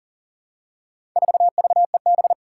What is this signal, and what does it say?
Morse code, heard as a keyed tone, about 41 wpm, 700 Hz. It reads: 4VEB
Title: Four Challenging Letter-Number Combinations 40wpm